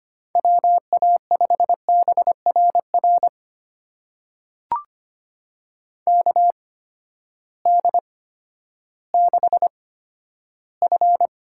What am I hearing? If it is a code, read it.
Morse code, heard as a keyed tone, about 25 wpm, spaced as for 13 wpm, 700 Hz. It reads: WA5BRR KD6F
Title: USA Call Signs - 8x Character Spacing - 25wpm